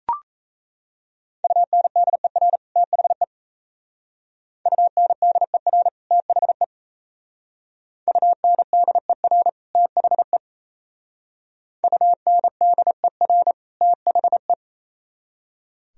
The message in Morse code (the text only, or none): UNDER THE UNDER THE UNDER THE UNDER THE